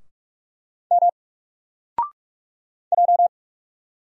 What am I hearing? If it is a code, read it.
Morse code, heard as a keyed tone, about 45 wpm, 700 Hz. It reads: M J
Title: Single Letter-Number 45wpm